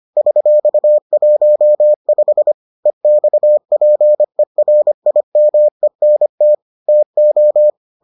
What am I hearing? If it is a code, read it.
Morse code, heard as a keyed tone, about 25 wpm, 600 Hz. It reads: $15 EXPERIMENT TO